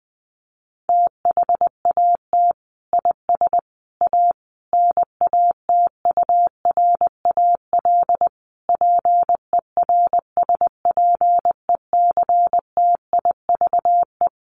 Morse code, heard as a keyed tone, about 19 wpm, 700 Hz. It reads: THAT IS A NATURAL PERSPECTIVE
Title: Sentences from Top 1200 Words 20wpm